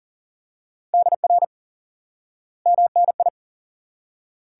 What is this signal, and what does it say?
Morse code, heard as a keyed tone, about 37 wpm, 700 Hz. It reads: DR MNI
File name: IQE Course 3 - Lesson 89 - YAGI - Two Elements - 40wpm